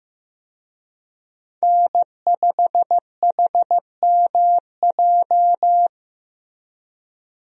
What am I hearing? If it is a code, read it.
Morse code, heard as a keyed tone, about 15 wpm, 700 Hz. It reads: N5HMJ
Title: USA Call Signs - 5 Characters - Repeated 3x - 15wpm